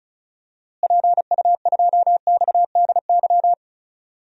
Morse code, heard as a keyed tone, about 36 wpm, 700 Hz. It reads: PU2XDY
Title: DXCC-100 Call Signs - 6 Characters - Repeated 2x - 35wpm